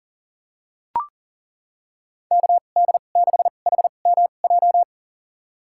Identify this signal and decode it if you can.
Morse code, heard as a keyed tone, about 38 wpm, spaced as for 28 wpm, 700 Hz. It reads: KD6HMJ
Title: USA Call Signs - 6 Characters - 2x Character Spacing - 40wpm